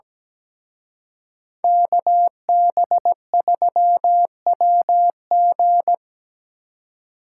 Morse code, heard as a keyed tone, about 20 wpm, 700 Hz. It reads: KB3WG
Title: Speed Racing - USA 2x2 Call Signs 17wpm